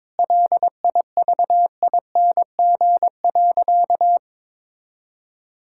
Morse code, heard as a keyed tone, about 22 wpm, 700 Hz. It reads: LIVING.
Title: Sentences from Top 1000 Words - Review Part 6 - 22wpm